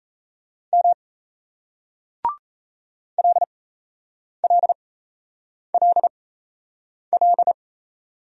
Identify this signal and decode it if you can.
Morse code, heard as a keyed tone, about 35 wpm, spaced as for 8 wpm, 700 Hz. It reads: M LLLL